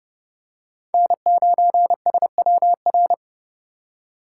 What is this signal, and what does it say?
Morse code, heard as a keyed tone, about 29 wpm, 700 Hz. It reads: N9SWR
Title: USA Call Signs - 4x Character Spacing - 30wpm